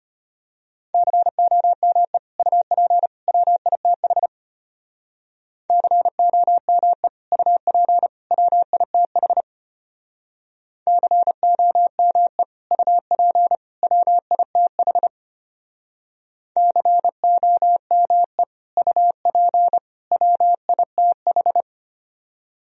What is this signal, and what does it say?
Morse code, heard as a keyed tone, about 33 wpm, 700 Hz. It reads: COME UP WITH COME UP WITH COME UP WITH COME UP WITH